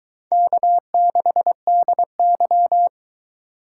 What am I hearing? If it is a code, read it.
Morse code, heard as a keyed tone, about 23 wpm, 700 Hz. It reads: K6DY